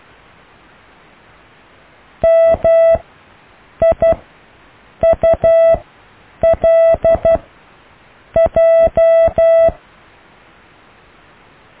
Morse code, heard as a keyed tone, about 12 wpm, 660 Hz. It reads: M I U L J
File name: cw / IJLMU